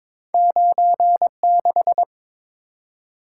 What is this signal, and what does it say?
Morse code, heard as a keyed tone, about 22 wpm, 700 Hz. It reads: 96
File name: practice / 22wpm / Two Letter-Number Combinations - 22wpm